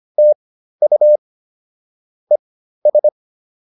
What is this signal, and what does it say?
Morse code, heard as a keyed tone, about 25 wpm, spaced as for 13 wpm, 600 Hz. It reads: TU ES